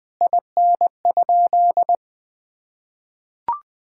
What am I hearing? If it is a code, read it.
Morse code, heard as a keyed tone, about 21 wpm, 700 Hz. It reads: IN?
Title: Sentences from Top 1000 Words - Review Part 6 - 20wpm